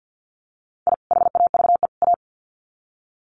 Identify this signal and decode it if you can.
Morse code, heard as a keyed tone, about 50 wpm, 700 Hz. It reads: I HAVE A
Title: Sets of 3 Words 50wpm